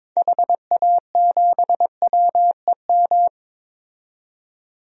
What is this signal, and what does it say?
Morse code, heard as a keyed tone, about 22 wpm, 700 Hz. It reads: HA7WEM
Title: DXCC-100 Call Signs - 6 Characters - 22wpm